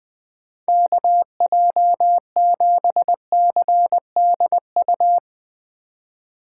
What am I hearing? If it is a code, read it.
Morse code, heard as a keyed tone, about 20 wpm, 700 Hz. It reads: KJ7CDU
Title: USA Call Signs - 6 Characters - 3x Character Spacing - 20wpm